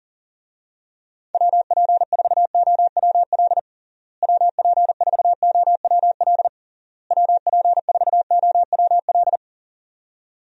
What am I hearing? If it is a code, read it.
Morse code, heard as a keyed tone, about 40 wpm, 700 Hz. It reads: WP4OWL WP4OWL WP4OWL
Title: USA Call Signs - 6 Characters - Repeated 3x - 40wpm